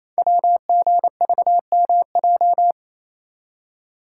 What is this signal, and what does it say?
Morse code, heard as a keyed tone, about 29 wpm, 700 Hz. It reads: WGVMJ
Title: Code Groups - 5 Letters - 28wpm